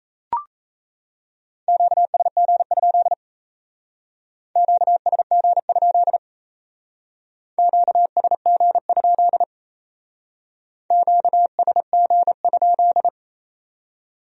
Morse code, heard as a keyed tone, about 35 wpm, 700 Hz. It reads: QSG? QSG? QSG? QSG?
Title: Speed-Racing Q-Codes - 28wpm